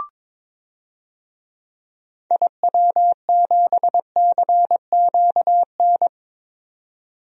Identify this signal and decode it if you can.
Morse code, heard as a keyed tone, about 22 wpm, 700 Hz. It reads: IW7CQN
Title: DXCC-100 Call Signs - 6 Characters - 22wpm